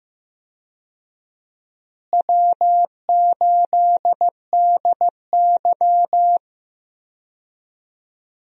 Morse code, heard as a keyed tone, about 15 wpm, 700 Hz. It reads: W8DY
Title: USA Call Signs - 4 Characters - Repeated 2x - 15wpm